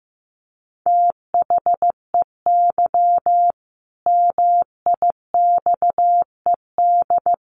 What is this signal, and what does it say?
Morse code, heard as a keyed tone, about 14 wpm, 700 Hz. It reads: THEY MIXED
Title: Sentences from Top 1500 Words 15wpm